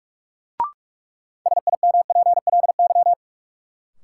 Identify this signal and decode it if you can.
Morse code, heard as a keyed tone, about 42 wpm, 700 Hz. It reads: SIMPLY
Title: Sentences from Top 600 Words - New Word Drill - 45wpm